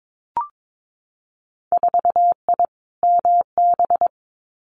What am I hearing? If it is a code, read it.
Morse code, heard as a keyed tone, about 22 wpm, 700 Hz. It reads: 4I MB
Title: Winter Field Day Exchanges Encore 22wpm